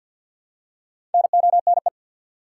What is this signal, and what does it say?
Morse code, heard as a keyed tone, about 53 wpm, 700 Hz. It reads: NONE
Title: Four Letter Words - 50wpm